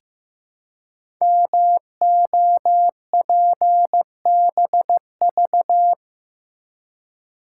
Morse code, heard as a keyed tone, about 15 wpm, 700 Hz. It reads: MOPBV
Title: Code Groups - 5 Letters - 15wpm